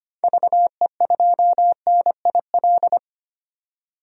Morse code, heard as a keyed tone, about 25 wpm, 700 Hz. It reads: VE2NIL
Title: DXCC-100 Call Signs - 6 Characters - 25wpm